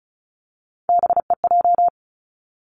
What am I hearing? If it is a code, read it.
Morse code, heard as a keyed tone, about 36 wpm, 700 Hz. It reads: BEJ
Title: Three Challenging Letter-Number Combinations 35wpm